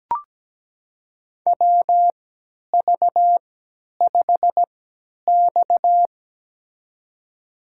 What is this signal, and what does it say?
Morse code, heard as a keyed tone, about 17 wpm, spaced as for 10 wpm, 700 Hz. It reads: WV5X
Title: USA Call Signs - 4 Characters - 3x Character Spacing - 17wpm